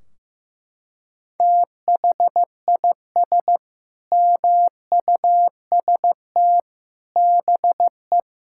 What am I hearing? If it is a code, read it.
Morse code, heard as a keyed tone, about 15 wpm, 700 Hz. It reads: THIS MUST BE
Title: Sentences from Top 1600 Words 15wpm